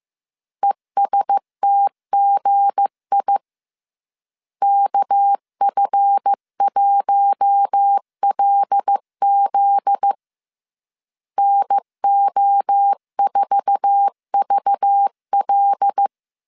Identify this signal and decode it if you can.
Morse code, heard as a keyed tone, about 15 wpm, 800 Hz. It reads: ESTGI KF1LZ NO4VL
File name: CW / DET / 1 9 1500C S C 14wpm 3245